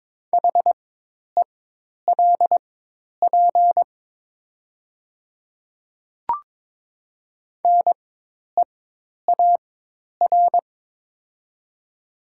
Morse code, heard as a keyed tone, about 22 wpm, spaced as for 10 wpm, 700 Hz. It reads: HELP NEAR
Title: 4x Character Spacing - Top 500 Words - 22wpm